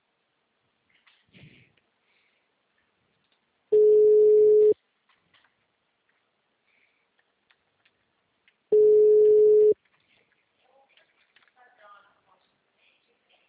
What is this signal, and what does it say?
Ringback tone (Europe, 425 Hz), with speech.
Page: Braganca - SP